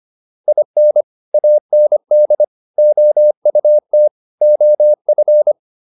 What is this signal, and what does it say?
Morse code, heard as a keyed tone, about 25 wpm, 600 Hz. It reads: IN AND OUT OF
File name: WD304 25